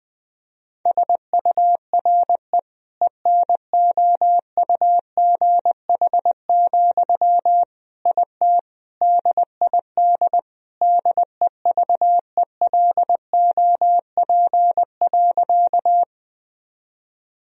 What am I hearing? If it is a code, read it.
Morse code, heard as a keyed tone, about 20 wpm, 700 Hz. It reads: SURE ENOUGH, IT DID DEVELOP.
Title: Sentences from Top 1000 Words - Review Part 2 - 20wpm